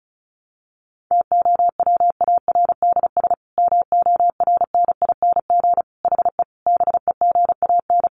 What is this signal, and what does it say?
Morse code, heard as a keyed tone, about 34 wpm, 700 Hz. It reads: TOWARDS MORNING HE BEGAN
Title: Sentences from Top 1100 Words 35wpm